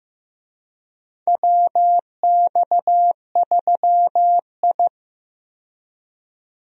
Morse code, heard as a keyed tone, about 15 wpm, 700 Hz. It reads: WX3I
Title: USA Call Signs - 4 Characters - 15wpm